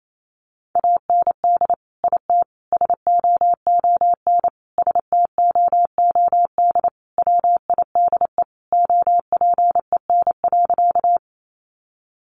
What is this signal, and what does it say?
Morse code, heard as a keyed tone, about 28 wpm, 700 Hz. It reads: AND IT SOON STOOD WIDE OPEN.
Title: Sentences from Top 900 Words 28wpm